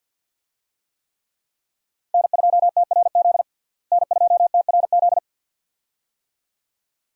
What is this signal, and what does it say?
Morse code, heard as a keyed tone, about 53 wpm, 700 Hz. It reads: N2TRZ N2TRZ
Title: USA Call Signs - 5 Characters - Repeated 2x - 50wpm